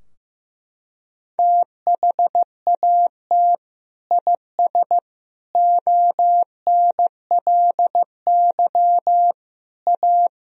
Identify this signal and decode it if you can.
Morse code, heard as a keyed tone, about 15 wpm, 700 Hz. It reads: THAT IS ONLY A
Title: Sentences from Top 1000 Words - Review Part 4 - 15wpm